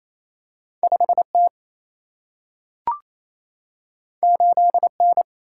Morse code, heard as a keyed tone, about 28 wpm, 700 Hz. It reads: 5T 8N
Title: Single-Letter Single-Number 28wpm